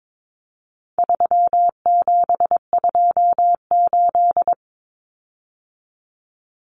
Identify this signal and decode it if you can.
Morse code, heard as a keyed tone, about 22 wpm, 700 Hz. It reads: 3728
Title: Common Cut Numbers - 4-Digit 22wpm